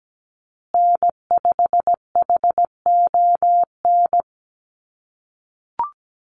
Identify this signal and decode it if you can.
Morse code, heard as a keyed tone, about 17 wpm, 700 Hz. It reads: N5HON
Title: USA Call Signs Repeated 2x 17wpm